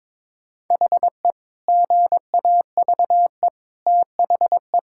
Morse code, heard as a keyed tone, about 22 wpm, 700 Hz. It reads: HE GAVE THE
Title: Sentences from Top 1600 Words 22wpm